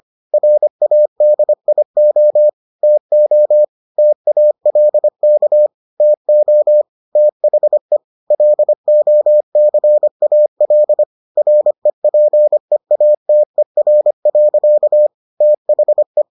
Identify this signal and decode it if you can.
Morse code, heard as a keyed tone, about 25 wpm, 600 Hz. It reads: RADIO TO TALK TO THE LOCAL REPEATER. THE